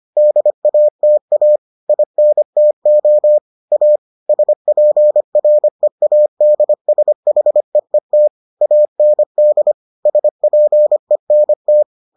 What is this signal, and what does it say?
Morse code, heard as a keyed tone, about 25 wpm, 600 Hz. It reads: DATA INTO A SPREADSHEET AND SPENT